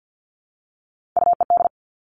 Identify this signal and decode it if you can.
Morse code, heard as a keyed tone, about 50 wpm, 700 Hz. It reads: UED
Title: All Cut Numbers - 3-Digit 50wpm